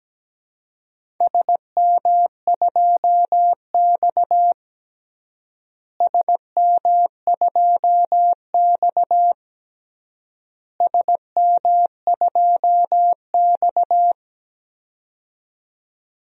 Morse code, heard as a keyed tone, about 17 wpm, 700 Hz. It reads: SM2X SM2X SM2X
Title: DXCC-100 Call Signs - 4 Characters - Repeated 3x - 17wpm